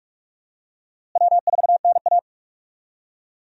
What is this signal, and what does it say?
Morse code, heard as a keyed tone, about 46 wpm, 700 Hz. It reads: W4NA
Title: USA Call Signs - 4 Characters - Repeated 2x - 45wpm